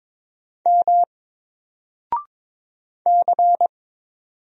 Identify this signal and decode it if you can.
Morse code, heard as a keyed tone, about 22 wpm, 700 Hz. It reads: M C
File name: Single-Letter-Number-22wpm